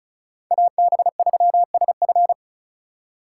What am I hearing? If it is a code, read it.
Morse code, heard as a keyed tone, about 43 wpm, 700 Hz. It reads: AB3SF
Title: Speed Racing - USA 2x2 Call Signs 35wpm